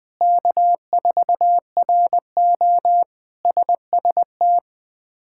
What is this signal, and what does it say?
Morse code, heard as a keyed tone, about 20 wpm, 700 Hz. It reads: K4RO SST
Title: K1USN SST QSO Practice - 20wpm